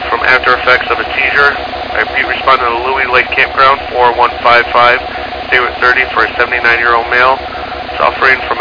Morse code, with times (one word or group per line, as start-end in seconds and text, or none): none